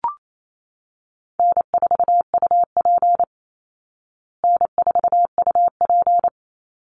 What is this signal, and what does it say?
Morse code, heard as a keyed tone, about 28 wpm, 700 Hz. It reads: N4UP N4UP
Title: Easy Call Signs Repeated 2x - 28wpm